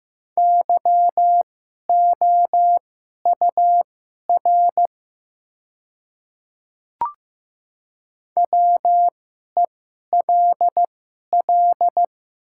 Morse code, heard as a keyed tone, about 15 wpm, spaced as for 11 wpm, 700 Hz. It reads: YOUR WELL
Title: 2x Character Spacing - Top 200 Words - 15wpm